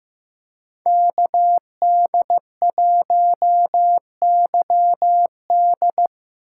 Morse code, heard as a keyed tone, about 15 wpm, 700 Hz. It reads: KD1YD
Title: USA Call Signs - 5 Characters - 2x Character Spacing - 15wpm